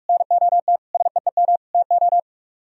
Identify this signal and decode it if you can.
Morse code, heard as a keyed tone, about 45 wpm, 700 Hz. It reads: NOT SEEM TO